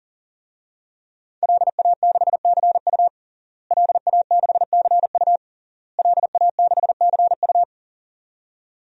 Morse code, heard as a keyed tone, about 37 wpm, 700 Hz. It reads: LA6CU LA6CU LA6CU
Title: DXCC-100 Call Signs - 5 Characters - Repeated 3x - 40wpm